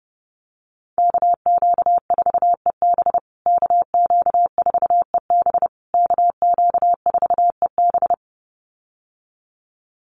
Morse code, heard as a keyed tone, about 30 wpm, 700 Hz. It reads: KQ4EB KQ4EB KQ4EB
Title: USA Call Signs Repeated 3x 30wpm